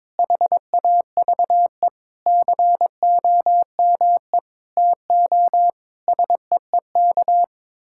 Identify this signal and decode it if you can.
Morse code, heard as a keyed tone, about 22 wpm, 700 Hz. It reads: HAVE COME TO SEEK
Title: Sentences from Top 1600 Words 22wpm